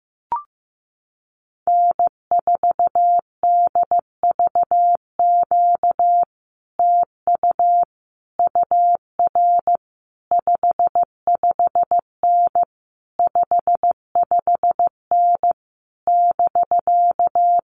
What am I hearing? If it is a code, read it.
Morse code, heard as a keyed tone, about 15 wpm, 700 Hz. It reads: N4DVQ TU UR 55N 55N <BK>
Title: Common QSO Phrases Encore 15wpm